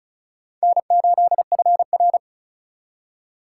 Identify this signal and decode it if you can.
Morse code, heard as a keyed tone, about 36 wpm, 700 Hz. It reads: N8FR
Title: USA Call Signs - 4 Characters - 2x Character Spacing - 35wpm